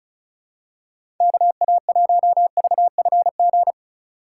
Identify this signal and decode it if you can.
Morse code, heard as a keyed tone, about 34 wpm, 700 Hz. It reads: KA1VFG
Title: USA Call Signs - 2x Character Spacing - 35wpm